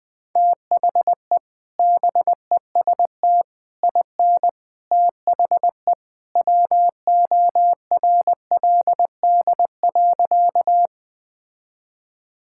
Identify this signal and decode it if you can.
Morse code, heard as a keyed tone, about 21 wpm, 700 Hz. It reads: THE BEST IN THE WORLD.
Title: Sentences with 3-Word Phrases - 20wpm